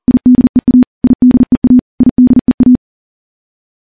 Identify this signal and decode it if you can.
Morse code, heard as a keyed tone, about 40 wpm, 250 Hz. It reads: IDEA IDEA IDEA
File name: morse-training-40WPM